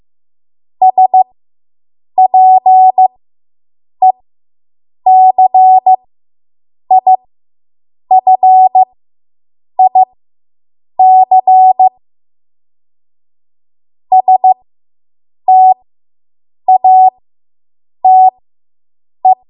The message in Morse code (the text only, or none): SPECIFIC STATE